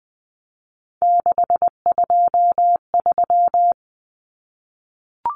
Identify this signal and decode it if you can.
Morse code, heard as a keyed tone, about 20 wpm, 700 Hz. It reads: 623
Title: Common Cut Numbers - 3-Digit 20wpm